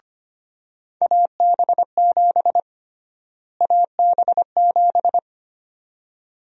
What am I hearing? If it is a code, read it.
Morse code, heard as a keyed tone, about 25 wpm, 700 Hz. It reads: AB7 AB7
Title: USA Call Sign Prefixes - Repeated 2x - 25wpm